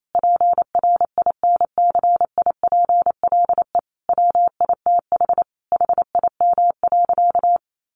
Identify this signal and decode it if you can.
Morse code, heard as a keyed tone, about 28 wpm, 700 Hz. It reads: PRINCIPLE WITH HIM.
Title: Sentences from Top 1400 Words 28wpm